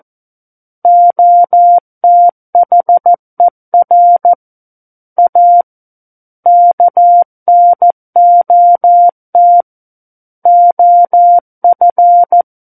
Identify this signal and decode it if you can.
Morse code, heard as a keyed tone, about 14 wpm, 700 Hz. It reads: OTHER A KNOT OF
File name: Book One / 05-Chapter 5 - 14 WPM